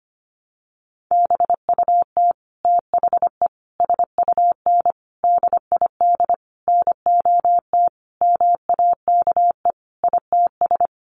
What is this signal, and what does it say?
Morse code, heard as a keyed tone, about 25 wpm, 700 Hz. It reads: BUT THE SUN DID NOT MAKE ITS
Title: Sentences from Top 1100 Words 25wpm